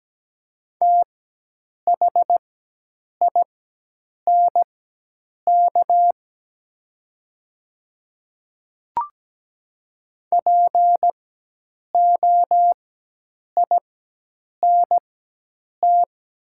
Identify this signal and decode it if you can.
Morse code, heard as a keyed tone, about 17 wpm, spaced as for 8 wpm, 700 Hz. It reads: THINK POINT
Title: 4x Character Spacing - Top 200 Words - 17wpm